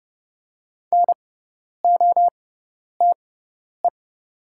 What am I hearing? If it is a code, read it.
Morse code, heard as a keyed tone, about 30 wpm, spaced as for 10 wpm, 690 Hz. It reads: NOTE
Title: 6x Character Spacing - Top 500 Words - 30wpm